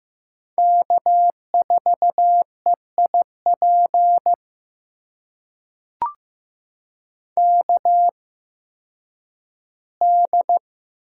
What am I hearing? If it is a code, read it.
Morse code, heard as a keyed tone, about 15 wpm, spaced as for 10 wpm, 700 Hz. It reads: K4EIP KD
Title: USA Call Signs - 8x Character Spacing - 15wpm